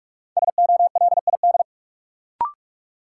Morse code, heard as a keyed tone, about 47 wpm, 700 Hz. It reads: SOLID
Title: Sentences from Top 1500 Words - New Word Drill - 45wpm